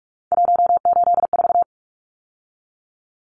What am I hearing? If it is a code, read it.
Morse code, heard as a keyed tone, about 44 wpm, 700 Hz. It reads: 184
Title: Numbers - 3-Digit 45wpm